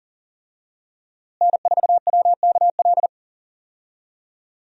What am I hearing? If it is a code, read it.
Morse code, heard as a keyed tone, about 40 wpm, 700 Hz. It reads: N4WKL
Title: USA Call Signs - 5 Characters - Repeated 3x - 40wpm